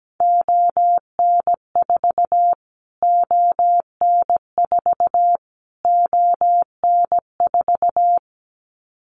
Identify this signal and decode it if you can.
Morse code, heard as a keyed tone, about 17 wpm, 700 Hz. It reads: ON4 ON4 ON4
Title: Callsign Prefixes for Top 25 DXCC - Repeated 3x - 17wpm